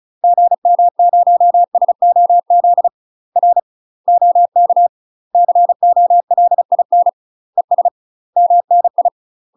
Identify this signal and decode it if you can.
Morse code, heard as a keyed tone, about 34 wpm, 700 Hz. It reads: GM0SOZ R OK COLIN ES MNI
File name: UK MASTER QSOs TRAINING / 35WPM / UK MASTER QSOs 0038